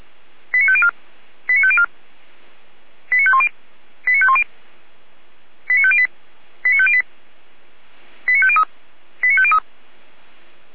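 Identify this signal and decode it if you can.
Five-tone selective call (ZVEI), heard as quick runs of five tones, tones of about 70 ms. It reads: 89504 89504 89510 89510 89508 89508 89503 89503
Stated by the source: ZVEI five-tone selective call received on SDR.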